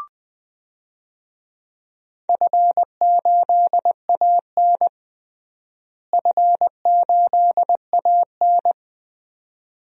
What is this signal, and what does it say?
Morse code, heard as a keyed tone, about 20 wpm, 700 Hz. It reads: F8AN F8AN
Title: DXCC-100 Call Signs - 4 Characters - Repeated 2x - 20wpm